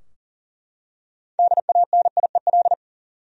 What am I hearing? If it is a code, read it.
Morse code, heard as a keyed tone, about 40 wpm, 700 Hz. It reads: DANIEL
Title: USA Common Names - 40wpm